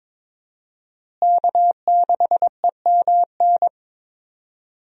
Morse code, heard as a keyed tone, about 22 wpm, 700 Hz. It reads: K6EMN
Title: USA Call Signs - 5 Characters - Repeated 2x - 22wpm